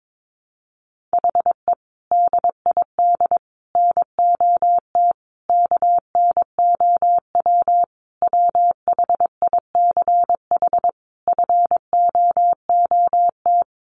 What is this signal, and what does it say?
Morse code, heard as a keyed tone, about 22 wpm, 700 Hz. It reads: HE DID NOT KNOW WHICH FOOT
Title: Sentences from Top 1300 Words 22wpm